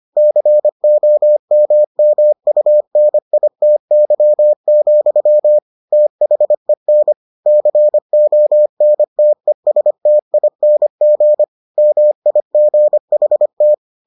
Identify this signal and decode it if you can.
Morse code, heard as a keyed tone, about 25 wpm, 600 Hz. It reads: COMMUNITY, THEN CONTESTING MIGHT